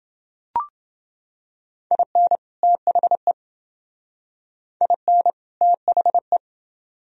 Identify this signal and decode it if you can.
Morse code, heard as a keyed tone, about 29 wpm, 700 Hz. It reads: IN THE IN THE